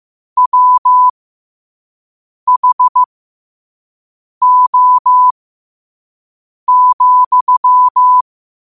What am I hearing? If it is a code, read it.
Morse code, heard as a keyed tone, about 15 wpm, spaced as for 5 wpm, 1000 Hz. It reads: WHO,
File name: qotdfiles / 5wpm